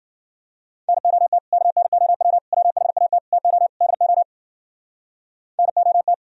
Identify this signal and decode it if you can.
Morse code, heard as a keyed tone, about 51 wpm, 700 Hz. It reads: NOT KNOW WHAT TO DO NOT